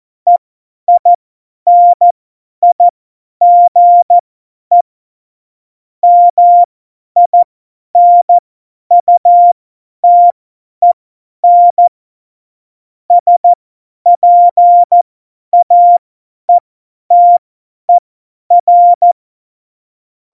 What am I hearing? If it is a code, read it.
Morse code, heard as a keyed tone, about 14 wpm, spaced as for 10 wpm, 700 Hz. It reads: EINIGE MINUTEN SPAETER
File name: Morsekurs / Witze / Witz 05. Januar 2021, 50 F70